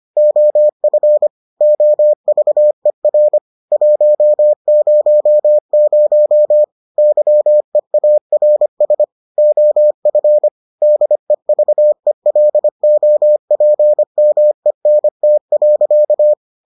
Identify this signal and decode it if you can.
Morse code, heard as a keyed tone, about 25 wpm, 600 Hz. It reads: OF OVER 100 YEARS OF DEVELOPMENT.